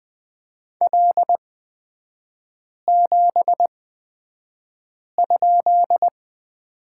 Morse code, heard as a keyed tone, about 20 wpm, spaced as for 5 wpm, 700 Hz. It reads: L7?
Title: 255 Lesson 41e - Period - Three Characters 20wpm